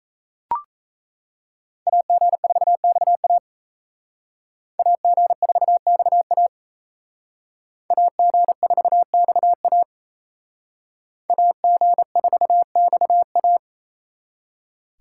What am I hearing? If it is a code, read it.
Morse code, heard as a keyed tone, about 36 wpm, 700 Hz. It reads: AG4XA AG4XA AG4XA AG4XA